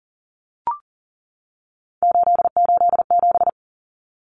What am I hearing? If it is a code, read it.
Morse code, heard as a keyed tone, about 40 wpm, 700 Hz. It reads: 887
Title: Common Cut Numbers - 3-Digit 40wpm